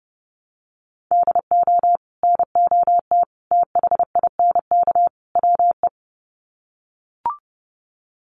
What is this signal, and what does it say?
Morse code, heard as a keyed tone, about 30 wpm, 700 Hz. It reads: DO NOT THINK WE
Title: Sets of 4 Words 30wpm